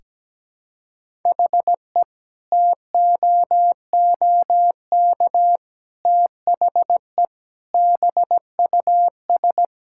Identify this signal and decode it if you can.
Morse code, heard as a keyed tone, about 17 wpm, 700 Hz. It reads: HE TOOK THE BUS
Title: Sentences from Top 1000 Words - Review Part 6 - 17wpm